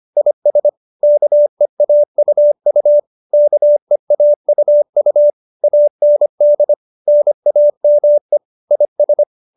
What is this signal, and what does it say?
Morse code, heard as a keyed tone, about 25 wpm, 590 Hz. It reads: IS KEAUU KEAUU AND NAME IS